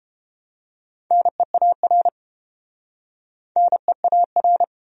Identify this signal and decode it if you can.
Morse code, heard as a keyed tone, about 31 wpm, 700 Hz. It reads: NEAR NEAR